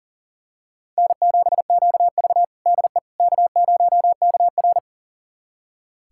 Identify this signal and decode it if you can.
Morse code, heard as a keyed tone, about 40 wpm, 700 Hz. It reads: N7QV DE K0KR
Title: USA Callsign DE Callsign - 4 Characters - 40wpm